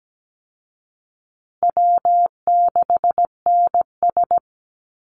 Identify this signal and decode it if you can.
Morse code, heard as a keyed tone, about 17 wpm, 700 Hz. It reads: W6NS
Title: Easy Call Signs Repeated 2x - 17wpm